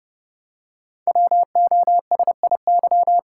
Morse code, heard as a keyed tone, about 29 wpm, 700 Hz. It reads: WOSIY
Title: Code Groups - 5 Letter:Numbers - 30wpm